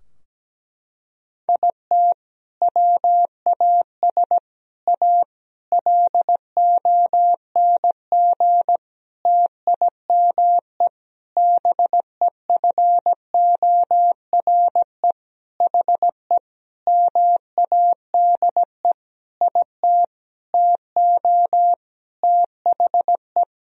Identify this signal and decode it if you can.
Morse code, heard as a keyed tone, about 17 wpm, 700 Hz. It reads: IT WAS A LONG TIME BEFORE HE MADE IT TO THE
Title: Sentences from Top 1000 Words - Review Part 2 - 17wpm